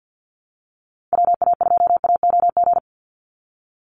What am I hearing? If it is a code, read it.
Morse code, heard as a keyed tone, about 50 wpm, 700 Hz. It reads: WA2AOG
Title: Call Signs - 50wpm